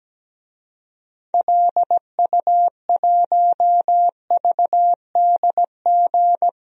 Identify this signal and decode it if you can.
Morse code, heard as a keyed tone, about 17 wpm, 700 Hz. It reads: LU1VDG
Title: DXCC-100 Call Signs - 6 Characters - Repeated 2x - 17wpm